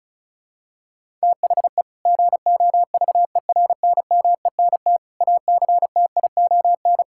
Morse code, heard as a keyed tone, about 36 wpm, 700 Hz. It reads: THE GOVERNMENT ACTION
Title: Sentences from Top 1000 Words - Review Part 5 - 35wpm